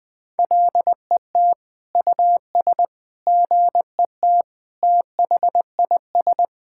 Morse code, heard as a keyed tone, about 20 wpm, 700 Hz. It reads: LET US GET THIS
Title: Sentences with 7-Word Phrases - 20wpm